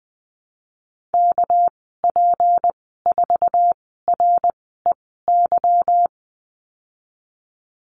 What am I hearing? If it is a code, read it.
Morse code, heard as a keyed tone, about 20 wpm, spaced as for 15 wpm, 700 Hz. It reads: KP4REY
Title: 200 Lesson 32g - Letter k - Call Signs 20wpm